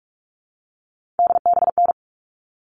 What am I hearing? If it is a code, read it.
Morse code, heard as a keyed tone, about 44 wpm, 700 Hz. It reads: DBN
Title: All Cut Numbers - 3-Digit 45wpm